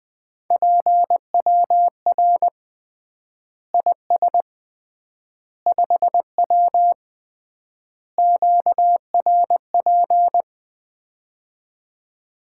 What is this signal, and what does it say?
Morse code, heard as a keyed tone, about 21 wpm, 700 Hz. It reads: PWR IS 5W QRP
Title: Common QSO Phrases - 3x Word-Spacing - 20wpm